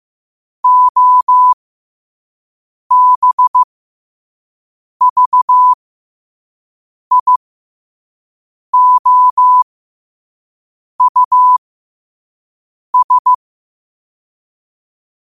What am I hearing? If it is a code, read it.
Morse code, heard as a keyed tone, about 15 wpm, spaced as for 5 wpm, 1000 Hz. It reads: OBVIOUS
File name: qotdfiles / 5wpm